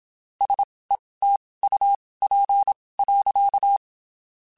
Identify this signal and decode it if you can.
Morse code, heard as a keyed tone, about 27 wpm, spaced as for 19 wpm, 800 Hz. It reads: SETUP.
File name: (spaced) / QSOs Spaced-mp3s / WD0BGP 1 @25 sp